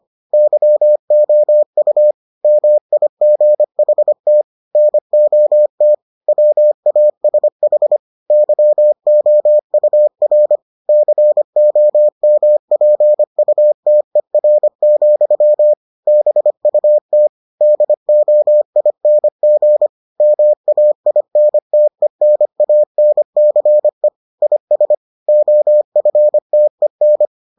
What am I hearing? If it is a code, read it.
Morse code, heard as a keyed tone, about 25 wpm, 600 Hz. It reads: YOU MIGHT NOT WASH YOUR COMPUTER, BUT DOING MAINTENANCE IS OFTEN